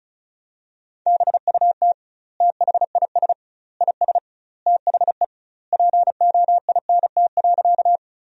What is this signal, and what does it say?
Morse code, heard as a keyed tone, about 36 wpm, 700 Hz. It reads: BUT THIS IS THE POINT.
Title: 2x Word-Spacing - Sentences from Top 200 Words - 35wpm